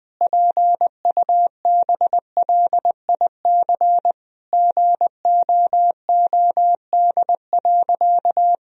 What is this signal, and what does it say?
Morse code, heard as a keyed tone, about 21 wpm, 700 Hz. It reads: PUBLIC GOOD.
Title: Sentences from Top 1000 Words - Review Part 3 - 20wpm